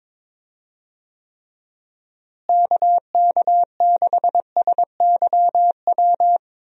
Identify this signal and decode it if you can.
Morse code, heard as a keyed tone, about 21 wpm, 700 Hz. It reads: KK6SYW
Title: USA Call Signs - 6 Characters - Repeated 3x - 22wpm